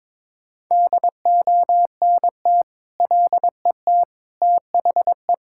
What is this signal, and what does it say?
Morse code, heard as a keyed tone, about 22 wpm, 700 Hz. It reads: DONT LET THE
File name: Sentences with 9-Word Phrases - 22wpm